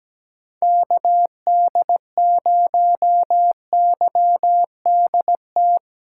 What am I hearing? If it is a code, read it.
Morse code, heard as a keyed tone, about 17 wpm, 700 Hz. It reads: KD0YDT
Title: USA Call Signs - 6 Characters - 8x Character Spacing - 17wpm